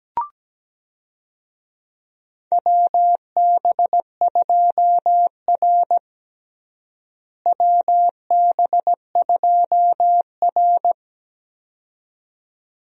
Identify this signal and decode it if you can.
Morse code, heard as a keyed tone, about 17 wpm, 700 Hz. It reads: WB2R WB2R
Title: USA Call Signs - 4 Characters - Repeated 2x - 17wpm